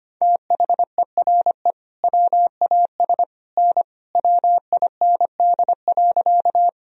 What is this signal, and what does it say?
Morse code, heard as a keyed tone, about 25 wpm, 700 Hz. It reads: THERE WAS N WIND.
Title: Sentences from Top 400 Words and Abbreviations 25wpm